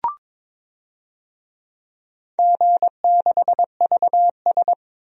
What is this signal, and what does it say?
Morse code, heard as a keyed tone, about 22 wpm, 700 Hz. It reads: G6VS
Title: DXCC-100 Call Signs - 4 Characters - Repeated 2x - 22wpm